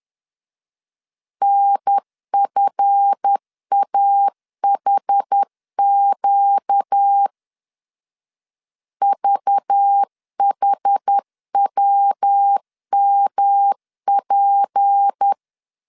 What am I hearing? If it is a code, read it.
Morse code, heard as a keyed tone, about 10 wpm, 800 Hz. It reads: NFAHQ VHWMP
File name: LEZIONI / 7 GF9Z / 7 GF9 10wpm 2747